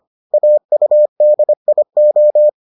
Morse code, heard as a keyed tone, about 25 wpm, 600 Hz. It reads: AUDIO